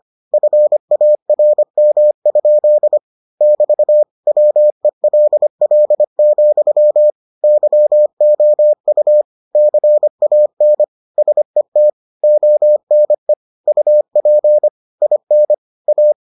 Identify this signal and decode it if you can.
Morse code, heard as a keyed tone, about 25 wpm, 600 Hz. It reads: FARM? <BT> WELL, YOU CAN SET ONE UP IN A